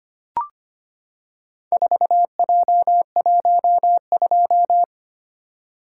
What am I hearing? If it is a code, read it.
Morse code, heard as a keyed tone, about 25 wpm, 700 Hz. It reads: 4J12
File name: practice / 25wpm / Four Letter-Number Combinations - 25wpm